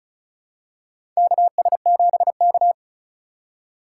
Morse code, heard as a keyed tone, about 36 wpm, 700 Hz. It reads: KS7K
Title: USA Call Signs - 4x Character Spacing - 35wpm